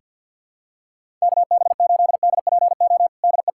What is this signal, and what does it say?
Morse code, heard as a keyed tone, about 49 wpm, 700 Hz. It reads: KB8DPO DE
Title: USA Callsign DE Callsign - 6 Characters - 50wpm